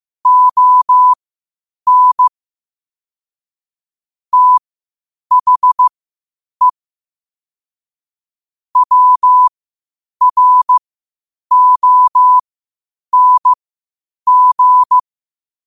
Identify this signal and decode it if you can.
Morse code, heard as a keyed tone, about 15 wpm, spaced as for 8 wpm, 1000 Hz. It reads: ON THE WRONG